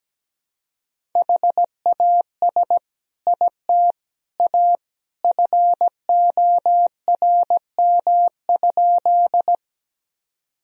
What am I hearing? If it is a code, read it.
Morse code, heard as a keyed tone, about 17 wpm, 700 Hz. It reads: HAS IT A FORM?
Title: Sentences from Top 1000 Words - Review Part 1 - 17wpm